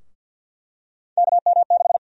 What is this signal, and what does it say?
Morse code, heard as a keyed tone, about 50 wpm, 700 Hz. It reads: KM6
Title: USA Call Sign Prefixes - 50wpm